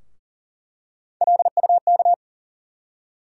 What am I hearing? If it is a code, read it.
Morse code, heard as a keyed tone, about 40 wpm, 700 Hz. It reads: LUK
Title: USA Call Sign Postfixes - 3 Letters 40wpm